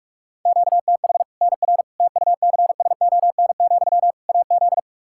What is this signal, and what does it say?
Morse code, heard as a keyed tone, about 42 wpm, 700 Hz. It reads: QTH NR TUCSON, AZ
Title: K1USN SST QSO Practice 45wpm